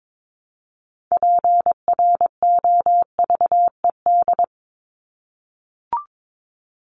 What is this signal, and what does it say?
Morse code, heard as a keyed tone, about 22 wpm, 700 Hz. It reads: PROVED
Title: ED Suffix - Top 100 Words with Base Words 22wpm